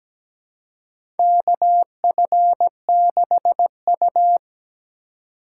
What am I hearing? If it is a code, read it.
Morse code, heard as a keyed tone, about 17 wpm, 700 Hz. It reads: KF6U
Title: USA Call Signs - 4 Characters - 3x Character Spacing - 17wpm